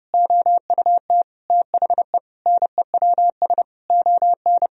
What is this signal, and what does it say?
Morse code, heard as a keyed tone, about 32 wpm, 700 Hz. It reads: OUT THE NEWS ON
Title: Sentences from Top 1000 Words - Review Part 4 - 30wpm